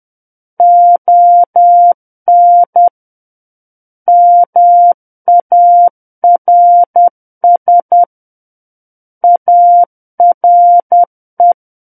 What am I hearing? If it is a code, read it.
Morse code, heard as a keyed tone, about 10 wpm, 700 Hz. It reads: ON MARS ARE
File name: Book One / 01-Chapter 1 - 10 WPM